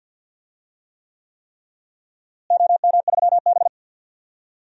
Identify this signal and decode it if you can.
Morse code, heard as a keyed tone, about 47 wpm, 700 Hz. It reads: OM3B
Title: DXCC-100 Call Signs - 4 Characters - Repeated 3x - 50wpm